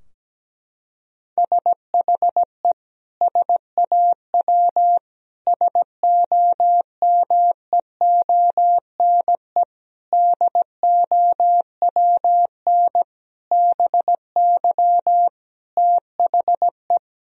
Morse code, heard as a keyed tone, about 17 wpm, 700 Hz. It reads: SHE SAW SOMEONE DOWN BY THE
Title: Sentences from Top 1000 Words - Review Part 4 - 17wpm